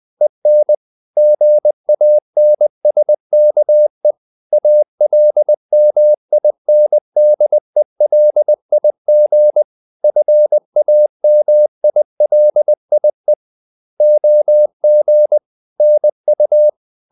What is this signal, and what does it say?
Morse code, heard as a keyed tone, about 20 wpm, 600 Hz. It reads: EN GANSKE ALMINDELIG FAMILIE OG NU
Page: Ferie 20 wpm